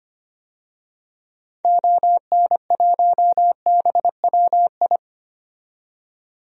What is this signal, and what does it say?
Morse code, heard as a keyed tone, about 25 wpm, 700 Hz. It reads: ON1BWI
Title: DXCC-100 Call Signs - 6 Characters - 25wpm